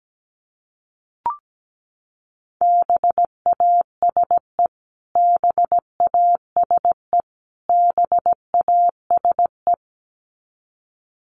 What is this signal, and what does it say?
Morse code, heard as a keyed tone, about 17 wpm, 700 Hz. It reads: BASE BASE BASE
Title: Top 500 Words Repeated 17wpm